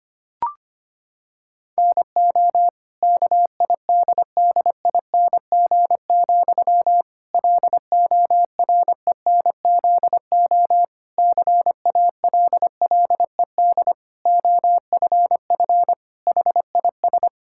Morse code, heard as a keyed tone, about 25 wpm, 700 Hz. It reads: NO KIDDING, LORENZO CALLED OFF HIS
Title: Pangram Challenge v2 - 25wpm